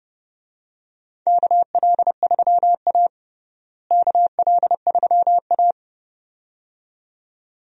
Morse code, heard as a keyed tone, about 30 wpm, 700 Hz. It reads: KL3A KL3A
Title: USA Call Signs - 4 Characters - Repeated 2x - 30wpm